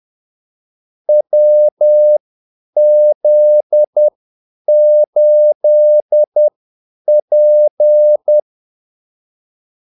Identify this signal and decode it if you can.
Morse code, heard as a keyed tone, about 10 wpm, spaced as for 8 wpm, 600 Hz. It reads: WZ8P
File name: Callsigns-10-8wpm